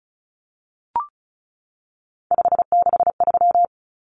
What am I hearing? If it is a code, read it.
Morse code, heard as a keyed tone, about 34 wpm, 700 Hz. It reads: 563
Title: Numbers - 3-Digit 35wpm